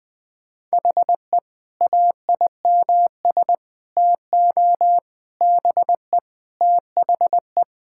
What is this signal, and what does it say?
Morse code, heard as a keyed tone, about 19 wpm, 700 Hz. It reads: HE AIMS TO BE THE
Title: Sentences with 8-Word Phrases - 20wpm